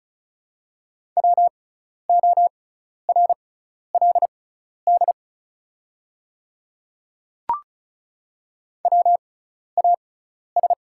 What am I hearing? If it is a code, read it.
Morse code, heard as a keyed tone, about 36 wpm, spaced as for 12 wpm, 690 Hz. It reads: WORLD WAS
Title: 6x Character Spacing - Top 200 Words - 35wpm